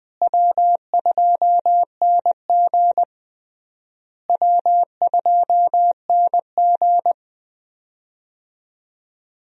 Morse code, heard as a keyed tone, about 21 wpm, 700 Hz. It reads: W2NG W2NG
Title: USA Call Signs - 4 Characters - Repeated 3x - 20wpm